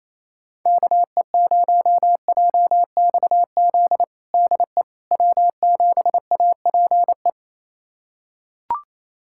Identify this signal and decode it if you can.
Morse code, heard as a keyed tone, about 28 wpm, 700 Hz. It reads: KE0JXZ DE W7APE
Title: USA Callsign DE Callsign 28wpm